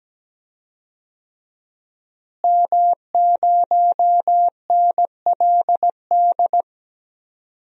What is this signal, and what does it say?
Morse code, heard as a keyed tone, about 17 wpm, 700 Hz. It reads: M0NLD
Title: DXCC-100 Call Signs - 5 Characters - Repeated 2x - 17wpm